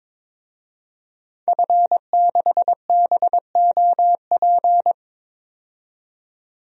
Morse code, heard as a keyed tone, about 21 wpm, 700 Hz. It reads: F6BOP
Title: DXCC-100 Call Signs - 5 Characters - 22wpm